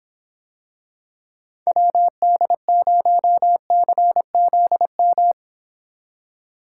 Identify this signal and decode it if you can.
Morse code, heard as a keyed tone, about 26 wpm, 700 Hz. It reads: WD0CZM